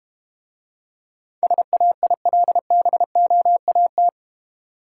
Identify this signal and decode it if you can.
Morse code, heard as a keyed tone, about 32 wpm, 700 Hz. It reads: SAILBOAT